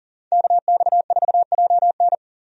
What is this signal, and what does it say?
Morse code, heard as a keyed tone, about 40 wpm, 700 Hz. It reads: KX4JN
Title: USA Call Signs - 5 Characters - 40wpm